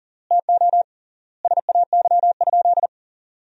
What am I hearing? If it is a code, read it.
Morse code, heard as a keyed tone, about 40 wpm, 700 Hz. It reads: TO SAY?
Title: 3x Word-Spacing - Sentences from Top 200 Words - 40wpm